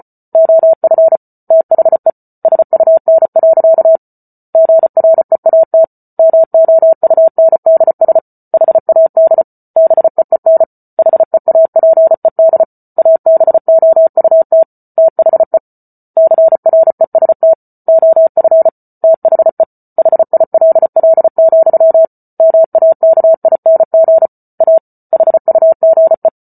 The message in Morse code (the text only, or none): OF THE SUN. GREAT MOUNDS HAD BEEN HEAPED ABOUT THE CREST OF THE HILL, MAKING A HUGE